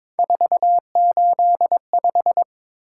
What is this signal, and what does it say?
Morse code, heard as a keyed tone, about 22 wpm, 700 Hz. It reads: 485
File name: practice / 22wpm / Three Letter-Number Combinations - 22wpm